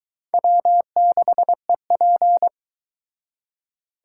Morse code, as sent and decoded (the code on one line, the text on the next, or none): .-- -.... .--.
W6EP